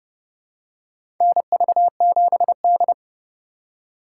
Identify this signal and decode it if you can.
Morse code, heard as a keyed tone, about 30 wpm, 700 Hz. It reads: NV7D
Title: USA Call Signs - 4 Characters - 3x Character Spacing - 30wpm